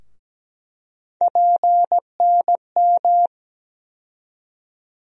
Morse code, heard as a keyed tone, about 17 wpm, 700 Hz. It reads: PNM
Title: USA Call Sign Postfixes - 3 Letters 17wpm